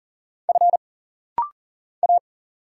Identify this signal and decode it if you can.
Morse code, heard as a keyed tone, about 40 wpm, 700 Hz. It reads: F A
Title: Single Letters - Rapid-Fire 40wpm